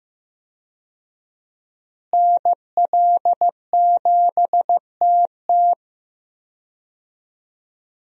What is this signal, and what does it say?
Morse code, heard as a keyed tone, about 15 wpm, 700 Hz. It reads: NL7TT
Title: USA Call Signs - 5 Characters - Repeated 2x - 15wpm